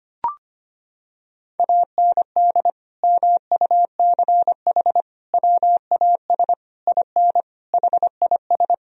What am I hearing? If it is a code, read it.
Morse code, heard as a keyed tone, about 25 wpm, 700 Hz. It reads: AND MUCH WAS IN HIS
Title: Sentences from Top 1000 Words - Review Part 2 - 25wpm